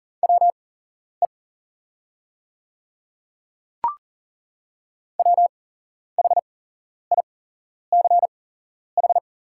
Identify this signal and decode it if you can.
Morse code, heard as a keyed tone, about 40 wpm, spaced as for 11 wpm, 690 Hz. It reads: WE WHICH
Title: 8x Character Spacing - Top 100 Words - 40wpm